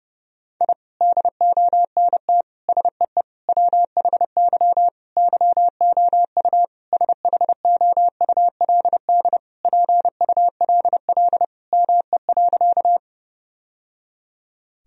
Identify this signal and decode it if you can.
Morse code, heard as a keyed tone, about 30 wpm, 700 Hz. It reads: I DONT SEE WHY YOU SHOULD PULL ME.
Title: Sentences from Top 1000 Words - Review Part 2 - 30wpm